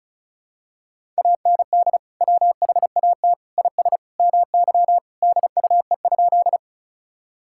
Morse code, heard as a keyed tone, about 36 wpm, 700 Hz. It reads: AND WHAT IS MY DUE?
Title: Sentences from Top 1000 Words - Review Part 6 - 35wpm